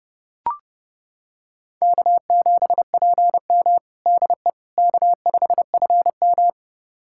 Morse code, heard as a keyed tone, about 30 wpm, 700 Hz. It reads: K7PM DE K5FM
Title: USA Callsign DE Callsign - 4 Characters - 30wpm